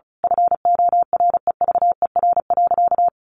Morse code, heard as a keyed tone, about 34 wpm, 700 Hz. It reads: FOREVER.
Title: A Sound Of Thunder 35wpm